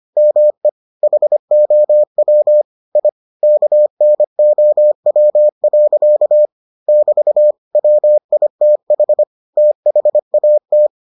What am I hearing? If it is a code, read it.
Morse code, heard as a keyed tone, about 25 wpm, 600 Hz. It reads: ME HOW I KNOW. <BT> WITH THAT